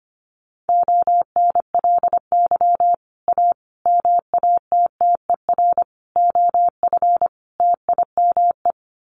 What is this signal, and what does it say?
Morse code, heard as a keyed tone, about 25 wpm, 700 Hz. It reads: ONLY A MATTER OF TIME
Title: Sets of 5 Words Encore 25wpm